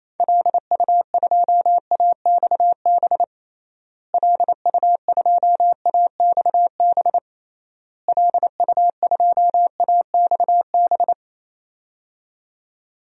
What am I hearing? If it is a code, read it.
Morse code, heard as a keyed tone, about 28 wpm, 700 Hz. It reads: LU2AXB LU2AXB LU2AXB
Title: DXCC-100 Call Signs - 6 Characters - Repeated 3x - 28wpm